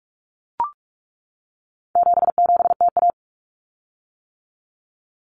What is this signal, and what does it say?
Morse code, heard as a keyed tone, about 44 wpm, 700 Hz. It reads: 77TA
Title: Common Cut Numbers - 4-Digit 45wpm